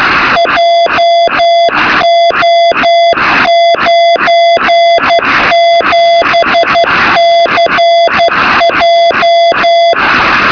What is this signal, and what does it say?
Morse code, heard as a keyed tone, about 12 wpm, 680 Hz, with noise. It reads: JO97CJ
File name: sm7lcb / 0423z sk1uhg 23cm